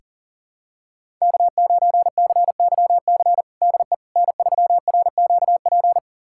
Morse code, heard as a keyed tone, about 37 wpm, 690 Hz. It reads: K9CYC DE N3RQP
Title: USA Callsign DE Callsign - 5 Characters - 40wpm